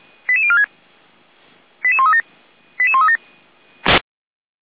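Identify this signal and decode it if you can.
Five-tone selective call (ZVEI), heard as quick runs of five tones, tones of about 70 ms. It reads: 88046 88137 88137
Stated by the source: ZVEI five-tone selective call 88136 received by SDR.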